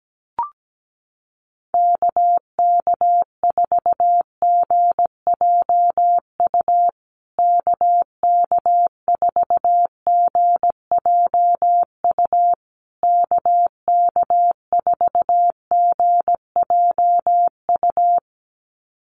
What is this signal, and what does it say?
Morse code, heard as a keyed tone, about 17 wpm, 700 Hz. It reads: KK4GJU KK4GJU KK4GJU
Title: USA Call Signs Repeated 3x 17wpm